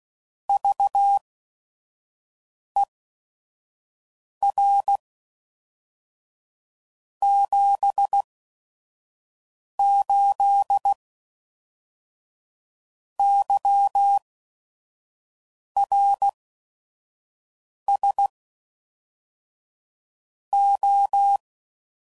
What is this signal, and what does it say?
Morse code, heard as a keyed tone, about 16 wpm, spaced as for 5 wpm, 800 Hz. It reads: VER 78 YRS O
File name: CW Code Groups @5WPM / QSOs / kc7haa